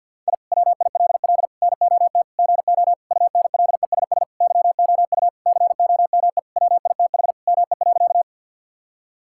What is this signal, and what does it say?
Morse code, heard as a keyed tone, about 50 wpm, 700 Hz. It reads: I WILL NOT GO UNLESS YOU COME WITH ME.
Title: Sentences from Top 1000 Words - Review Part 6 - 50wpm